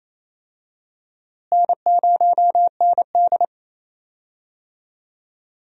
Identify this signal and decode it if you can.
Morse code, heard as a keyed tone, about 28 wpm, 700 Hz. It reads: N0ND
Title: USA Call Signs - 4 Characters - Repeated 3x - 28wpm